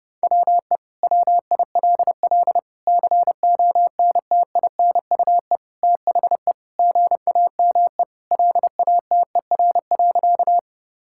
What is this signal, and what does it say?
Morse code, heard as a keyed tone, about 30 wpm, 700 Hz. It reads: WE WILL CONTINUE THE GAME LATER.
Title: Sentences from Top 1000 Words - Review Part 6 - 30wpm